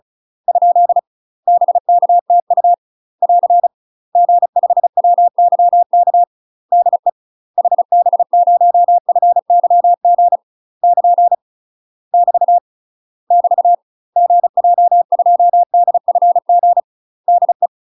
Morse code, heard as a keyed tone, about 33 wpm, 700 Hz. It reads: ? BKTU <AR> G5WYK DE HB0FYG <KN> <BT> <BT> GJ2DFG DE
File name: UK MASTER QSOs TRAINING / 35WPM / UK MASTER QSOs 0017